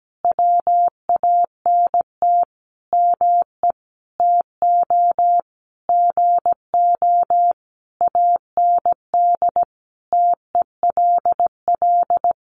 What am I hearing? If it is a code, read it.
Morse code, heard as a keyed tone, about 17 wpm, 700 Hz. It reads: WANT ME TO GO AND TELL
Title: Sentences from Top 200 Words 17wpm